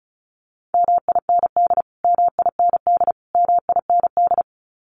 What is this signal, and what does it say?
Morse code, heard as a keyed tone, about 34 wpm, 700 Hz. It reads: MIND MIND MIND
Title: Top 500 Words Repeated 35wpm